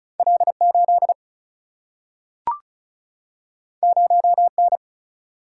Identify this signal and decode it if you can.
Morse code, heard as a keyed tone, about 36 wpm, 700 Hz. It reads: L 8 0 N
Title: Single-Letter Single-Number 35wpm